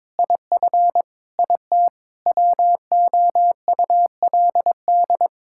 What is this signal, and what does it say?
Morse code, heard as a keyed tone, about 22 wpm, 700 Hz. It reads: IF IT WOULD
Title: Sentences from Top 1600 Words 22wpm